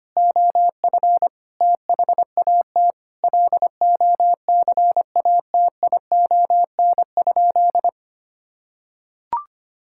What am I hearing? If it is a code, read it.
Morse code, heard as a keyed tone, about 25 wpm, 700 Hz. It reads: OF THAT LOCATION?
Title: Sentences from Top 1000 Words - Review Part 6 - 25wpm